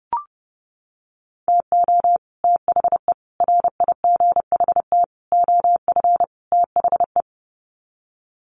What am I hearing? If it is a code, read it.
Morse code, heard as a keyed tone, about 30 wpm, 700 Hz. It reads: TO THE RIGHT OF THE
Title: Sets of 5 Words Encore 30wpm